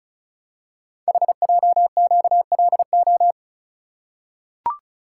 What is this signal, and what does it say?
Morse code, heard as a keyed tone, about 34 wpm, 690 Hz. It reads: HJQLO
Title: Code Groups - 5 Letters - 35wpm